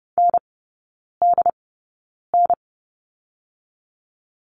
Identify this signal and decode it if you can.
Morse code, heard as a keyed tone, about 30 wpm, spaced as for 9 wpm, 690 Hz. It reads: NDN
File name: Lesson 17e - Letter u - Three Characters 30wpm